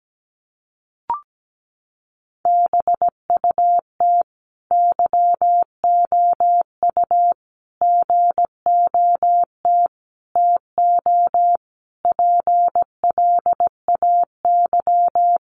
Morse code, heard as a keyed tone, about 17 wpm, 700 Hz. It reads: BUT YOU GOT TO PLAY
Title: Sentences from Top 300 Words 17wpm